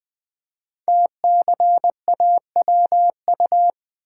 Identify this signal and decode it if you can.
Morse code, heard as a keyed tone, about 20 wpm, 700 Hz. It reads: TCAWU
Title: Code Groups - 5 Letter:Numbers - 20wpm